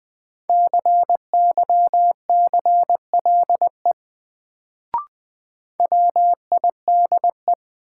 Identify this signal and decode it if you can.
Morse code, heard as a keyed tone, about 20 wpm, 700 Hz. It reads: CYCLE WIDE
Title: Sentences from Top 900 Words - New Word Drill - 20wpm